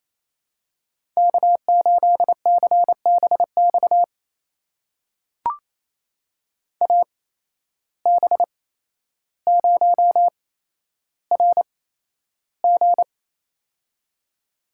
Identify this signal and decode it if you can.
Morse code, heard as a keyed tone, about 28 wpm, spaced as for 12 wpm, 700 Hz. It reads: K8CBX AB0RG
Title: USA Call Signs - 5 Characters - 8x Character Spacing - 28wpm